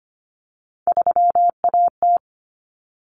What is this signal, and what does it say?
Morse code, heard as a keyed tone, about 25 wpm, 700 Hz. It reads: 3AT
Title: Common Cut Numbers - 3-Digit 25wpm